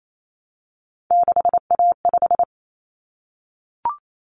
Morse code, heard as a keyed tone, about 28 wpm, 700 Hz. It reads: 6A5
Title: Common Cut Numbers - 3-Digit 28wpm